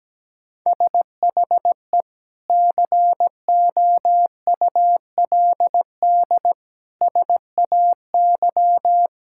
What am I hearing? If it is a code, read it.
Morse code, heard as a keyed tone, about 17 wpm, 700 Hz. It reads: SHE COULD SAY
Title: Sentences from Top 1600 Words 17wpm